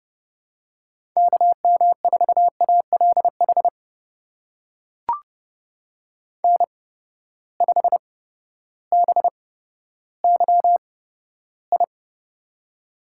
Morse code, heard as a keyed tone, about 30 wpm, spaced as for 14 wpm, 700 Hz. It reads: KM4ALH N5BYI
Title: USA Call Signs - 8x Character Spacing - 30wpm